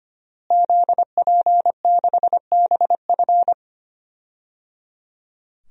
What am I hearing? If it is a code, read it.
Morse code, heard as a keyed tone, about 25 wpm, 700 Hz. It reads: ZP6BF
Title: DXCC-100 Call Signs - 5 Characters - Repeated 3x - 25wpm